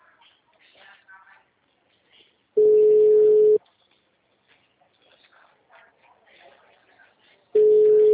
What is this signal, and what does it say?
Ringback tone (Europe, 425 Hz), with speech.